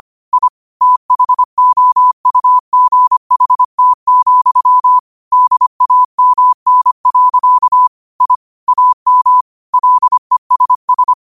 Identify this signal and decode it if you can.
Morse code, heard as a keyed tone, about 25 wpm, 1000 Hz. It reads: I THOUGHT, DAMN. I AM LESS
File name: qotdfiles / 25wpm